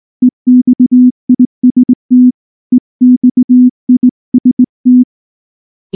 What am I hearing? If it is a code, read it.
Morse code, heard as a keyed tone, about 19 wpm, 250 Hz. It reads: EXIST EXIST
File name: morse / morse-training-20WPM-8